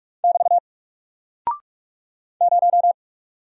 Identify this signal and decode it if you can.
Morse code, heard as a keyed tone, about 45 wpm, 700 Hz. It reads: <BT> 0
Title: Single Character (Letter:Number:Punctuation:Prosign) Alt 45wpm